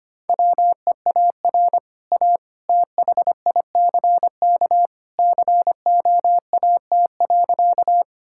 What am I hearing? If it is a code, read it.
Morse code, heard as a keyed tone, about 25 wpm, 690 Hz. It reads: WEAR A THICK COAT.
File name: Easy Sentences with Hints - 25wpm